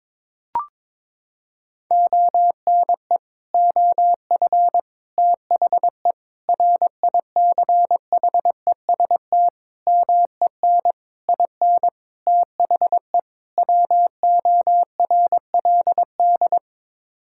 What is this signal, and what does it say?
Morse code, heard as a keyed tone, about 22 wpm, 700 Hz. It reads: ONE OF THE RICHEST MEN IN THE WORLD
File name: practice / 22wpm / Sets of 8 Words - 22wpm